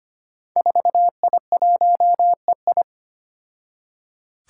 Morse code, heard as a keyed tone, about 25 wpm, 700 Hz. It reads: 4I1EI
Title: Five Challenging Letter-Number Combinations 25wpm